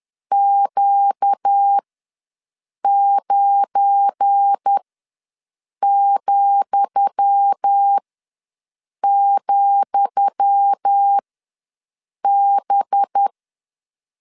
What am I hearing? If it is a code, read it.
Morse code, heard as a keyed tone, about 10 wpm, spaced as for 6 wpm, 800 Hz. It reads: Q9,,B
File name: LEZIONI / 10 SIGNS / 10D COMA ws 10wpm 4314